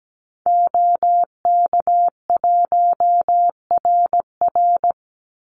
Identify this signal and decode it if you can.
Morse code, heard as a keyed tone, about 17 wpm, 700 Hz. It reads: OK1RR
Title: Call Signs - 17wpm